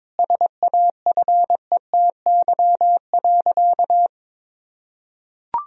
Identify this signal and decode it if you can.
Morse code, heard as a keyed tone, about 22 wpm, 700 Hz. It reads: SAFETY.
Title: Sentences from Top 1000 Words - Review Part 4 - 22wpm